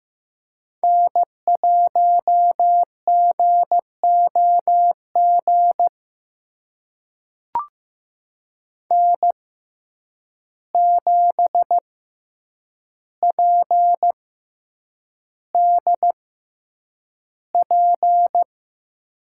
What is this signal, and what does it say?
Morse code, heard as a keyed tone, about 15 wpm, spaced as for 8 wpm, 700 Hz. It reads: N1GOG N7PDP
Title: USA Call Signs - 5 Characters - 6x Character Spacing - 15wpm